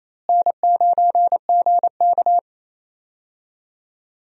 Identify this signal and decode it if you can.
Morse code, heard as a keyed tone, about 27 wpm, 700 Hz. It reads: N9GK
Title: USA Call Signs - 4 Characters - Repeated 3x - 28wpm